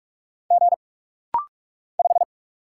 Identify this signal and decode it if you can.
Morse code, heard as a keyed tone, about 44 wpm, 700 Hz. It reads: G 5
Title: Single Letter-Number - Rapid-Fire 45wpm